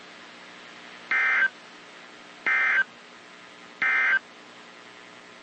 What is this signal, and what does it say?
Emergency Alert System (SAME) tones, heard as harsh buzzing bursts. It is heard from 1.0 to 4.3 s.
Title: WZ2523 - NOAA Weather Radio